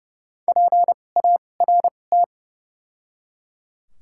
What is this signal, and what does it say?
Morse code, heard as a keyed tone, about 30 wpm, spaced as for 22 wpm, 700 Hz. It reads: PART
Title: 2x Character Spacing - Top 200 Words - 30wpm